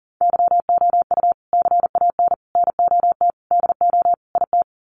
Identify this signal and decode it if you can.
Morse code, heard as a keyed tone, about 40 wpm, 700 Hz. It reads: YOU CAN NOT DO IT
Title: Sets of 5 Words Encore2 40wpm